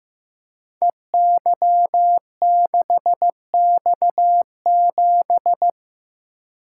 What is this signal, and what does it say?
Morse code, heard as a keyed tone, about 15 wpm, 700 Hz. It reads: EY6X7
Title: Code Groups - 5 Letter:Numbers - 15wpm